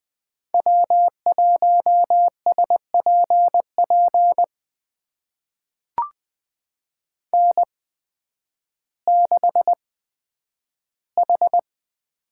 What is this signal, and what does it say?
Morse code, heard as a keyed tone, about 20 wpm, spaced as for 11 wpm, 700 Hz. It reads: W1SPP N6H
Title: USA Call Signs - 5 Characters - 8x Character Spacing - 20wpm